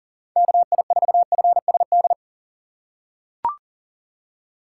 Morse code, heard as a keyed tone, about 38 wpm, 700 Hz. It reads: KI4FSD
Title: USA Call Signs - 6 Characters - 2x Character Spacing - 40wpm